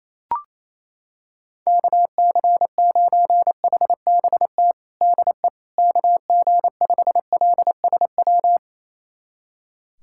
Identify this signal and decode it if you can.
Morse code, heard as a keyed tone, about 28 wpm, 700 Hz. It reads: KC9HBT DE KG5LSW
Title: USA Callsign DE Callsign 28wpm